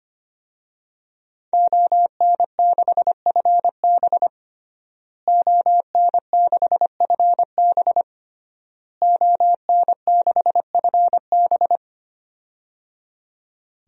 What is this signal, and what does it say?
Morse code, heard as a keyed tone, about 25 wpm, 700 Hz. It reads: ON6FB ON6FB ON6FB
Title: DXCC-100 Call Signs - 5 Characters - Repeated 3x - 25wpm